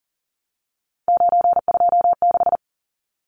Morse code, heard as a keyed tone, about 40 wpm, 700 Hz. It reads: 926
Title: Numbers - 3-Digit 40wpm